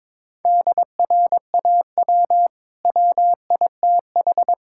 Morse code, heard as a keyed tone, about 21 wpm, 700 Hz. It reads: DRAW WITH
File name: practice / 22wpm / Easy Sentences with Hints - 22wpm